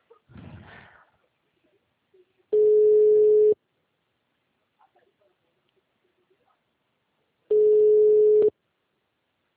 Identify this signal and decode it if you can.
Ringback tone (Europe, 425 Hz).